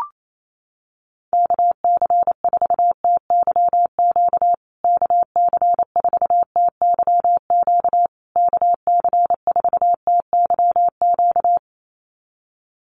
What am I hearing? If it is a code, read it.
Morse code, heard as a keyed tone, about 28 wpm, 700 Hz. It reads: KC4TYQ KC4TYQ KC4TYQ
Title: USA Call Signs Repeated 3x 28wpm